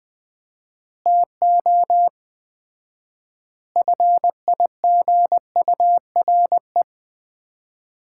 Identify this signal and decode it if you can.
Morse code, heard as a keyed tone, about 20 wpm, 700 Hz. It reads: TO FIGURE
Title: 4x Word-Spacing - Sets of 5 Words - 20wpm